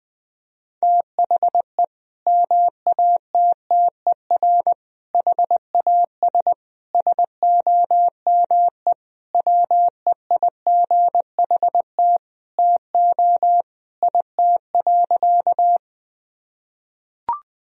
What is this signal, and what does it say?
Morse code, heard as a keyed tone, about 20 wpm, 700 Hz. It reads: THE MATTER HAS SOME WEIGHT TO IT.
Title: Sentences from Top 1000 Words - Review Part 3 - 20wpm